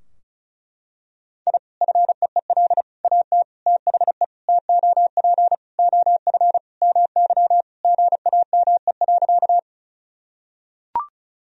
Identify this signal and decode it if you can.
Morse code, heard as a keyed tone, about 36 wpm, 700 Hz. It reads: I FEEL AT THE TOP OF MY GAME.
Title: Sentences from Top 1000 Words - Review Part 2 - 35wpm